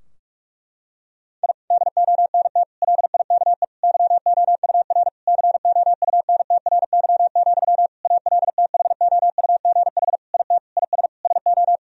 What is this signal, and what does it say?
Morse code, heard as a keyed tone, about 46 wpm, 700 Hz. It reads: I DONT LIKE YOUR COUNTRY, ALTHOUGH IT IS SO
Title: Sentences from Top 1000 Words - Review Part 3 - 45wpm